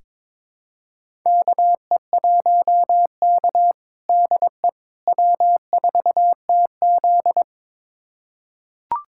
Morse code, heard as a keyed tone, about 21 wpm, 700 Hz. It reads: KE1K DE W4TZ
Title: USA Callsign DE Callsign - 4 Characters - 22wpm